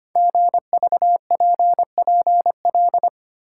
Morse code, heard as a keyed tone, about 25 wpm, 700 Hz. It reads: GVPPL
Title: Code Groups - 5 Letters - 25wpm